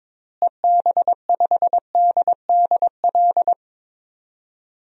Morse code, heard as a keyed tone, about 22 wpm, 700 Hz. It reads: EB5DDL
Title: DXCC-100 Call Signs - 6 Characters - Repeated 2x - 22wpm